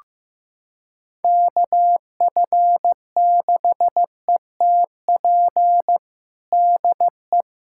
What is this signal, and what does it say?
Morse code, heard as a keyed tone, about 15 wpm, 700 Hz. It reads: KF6ETP DE
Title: USA Callsign DE Callsign - 6 Characters - 15wpm